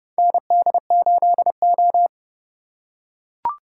Morse code, heard as a keyed tone, about 30 wpm, 700 Hz. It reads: ND8O
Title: USA Call Signs - 4 Characters - 2x Character Spacing - 30wpm